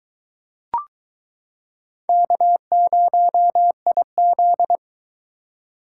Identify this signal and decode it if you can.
Morse code, heard as a keyed tone, about 22 wpm, 700 Hz. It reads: K0IZ